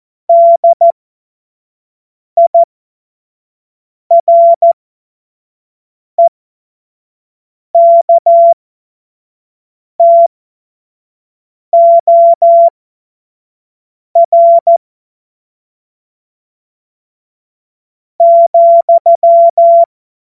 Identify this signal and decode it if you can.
Morse code, heard as a keyed tone, about 14 wpm, spaced as for 4 wpm, 680 Hz. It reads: DIREKTOR,